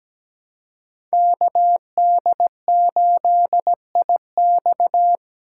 Morse code, heard as a keyed tone, about 17 wpm, 700 Hz. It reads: KD8IX
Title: USA Call Signs - 4x Character Spacing - 17wpm